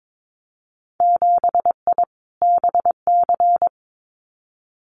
Morse code, heard as a keyed tone, about 22 wpm, 700 Hz. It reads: 7I BC
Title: Winter Field Day Exchanges Encore 22wpm